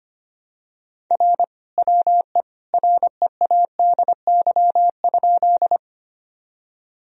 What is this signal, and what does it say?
Morse code, heard as a keyed tone, about 25 wpm, 700 Hz. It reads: R WE READY?
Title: Sentences from Top 400 Words and Abbreviations 25wpm